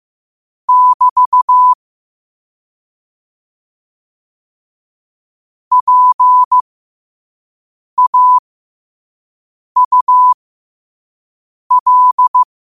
Morse code, heard as a keyed tone, about 15 wpm, spaced as for 5 wpm, 1000 Hz. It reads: <BT> PAUL
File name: qotdfiles / 5wpm